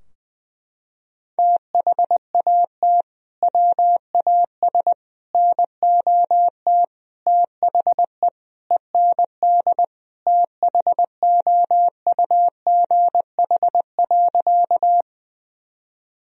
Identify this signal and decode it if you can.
Morse code, heard as a keyed tone, about 20 wpm, 700 Hz. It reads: THAT WAS NOT THE END THOUGH.
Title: Sentences from Top 1000 Words - Review Part 2 - 20wpm